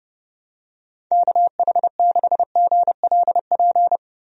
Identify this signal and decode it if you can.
Morse code, heard as a keyed tone, about 31 wpm, 700 Hz. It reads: KH6GLP
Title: DXCC-100 Call Signs - 6 Characters - Repeated 2x - 30wpm